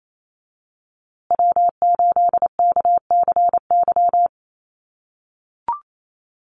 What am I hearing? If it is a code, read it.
Morse code, heard as a keyed tone, about 28 wpm, 700 Hz. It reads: W8KCY
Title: USA Call Signs Repeated 2x 28wpm